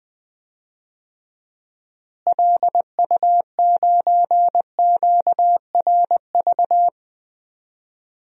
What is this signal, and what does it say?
Morse code, heard as a keyed tone, about 20 wpm, 700 Hz. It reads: LU9QRV
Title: DXCC-100 Call Signs - 6 Characters - Repeated 2x - 20wpm